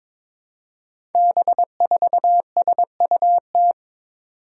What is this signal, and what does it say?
Morse code, heard as a keyed tone, about 22 wpm, 700 Hz. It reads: B4SUT
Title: Five Challenging Letter-Number Combinations 22wpm